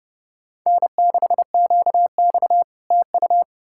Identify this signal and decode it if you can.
Morse code, heard as a keyed tone, about 40 wpm, 700 Hz. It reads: N6QX TU